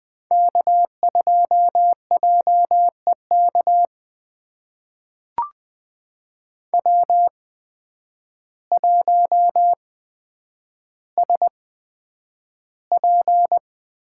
Morse code, heard as a keyed tone, about 20 wpm, spaced as for 9 wpm, 700 Hz. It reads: K2JEK W1SP
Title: USA Call Signs - 5 Characters - 8x Character Spacing - 20wpm